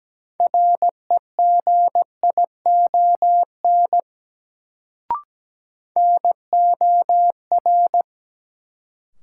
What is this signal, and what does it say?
Morse code, heard as a keyed tone, about 17 wpm, 700 Hz. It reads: REGION NOR
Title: Sentences from Top 1500 Words - New Word Drill Alt - 17wpm